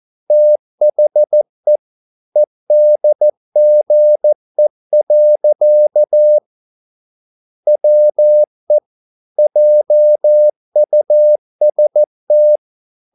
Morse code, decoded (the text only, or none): THE EDGE. WE JUST